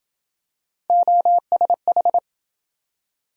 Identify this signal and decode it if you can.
Morse code, heard as a keyed tone, about 27 wpm, 700 Hz. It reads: OSH